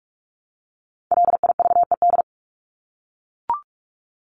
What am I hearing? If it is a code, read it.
Morse code, heard as a keyed tone, about 44 wpm, 700 Hz. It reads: LIVED
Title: ED Suffix - Top 100 Words with Base Words 45wpm